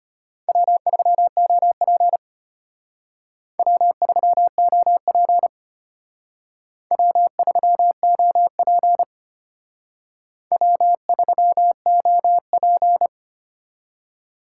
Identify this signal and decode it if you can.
Morse code, heard as a keyed tone, about 32 wpm, 700 Hz. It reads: W3OP W3OP W3OP W3OP